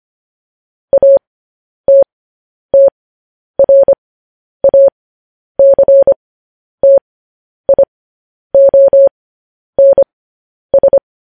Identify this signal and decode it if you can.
Morse code, heard as a keyed tone, about 25 wpm, spaced as for 10 wpm, 550 Hz. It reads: ATTRACTIONS
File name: podcast / News headlines-1x-25x10